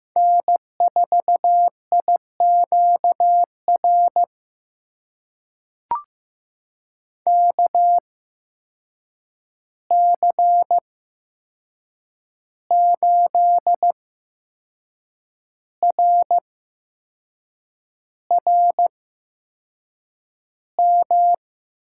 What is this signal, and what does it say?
Morse code, heard as a keyed tone, about 15 wpm, spaced as for 6 wpm, 700 Hz. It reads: N4IQR KC8RRM
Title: USA Call Signs - 8x Character Spacing - 15wpm